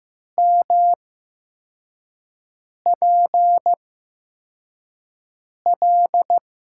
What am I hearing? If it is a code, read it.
Morse code, heard as a keyed tone, about 15 wpm, spaced as for 4 wpm, 700 Hz. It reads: MPL
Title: USA Call Signs - 8x Character Spacing - 15wpm